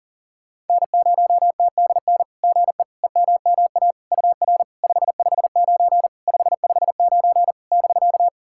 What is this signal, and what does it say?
Morse code, heard as a keyed tone, about 40 wpm, 700 Hz. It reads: N0TDN GE EMMA UR 559 559 <BK>
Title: SOTA QSO Practice 40wpm